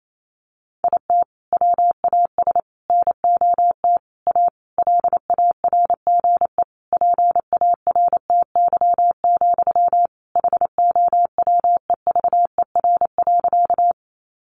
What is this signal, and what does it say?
Morse code, heard as a keyed tone, about 28 wpm, 700 Hz. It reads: IT WAS NOT A LARGE PARTY, HOWEVER.
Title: Sentences from Top 1400 Words 28wpm